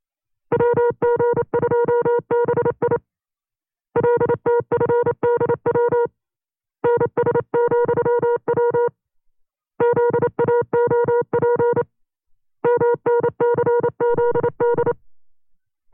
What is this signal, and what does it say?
Morse code, heard as a keyed tone, about 28 wpm, 480 Hz. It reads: WG2BI LTFDW NS,W ZAOP MNCZD
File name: CW-practice/25WPM